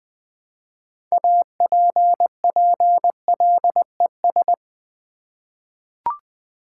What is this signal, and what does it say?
Morse code, heard as a keyed tone, about 20 wpm, 700 Hz. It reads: APPLES
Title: Easy Words with Hints - 20wpm